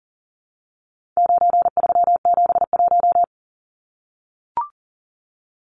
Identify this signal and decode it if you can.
Morse code, heard as a keyed tone, about 40 wpm, 700 Hz. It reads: 9371
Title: Numbers - 4-Digit 40wpm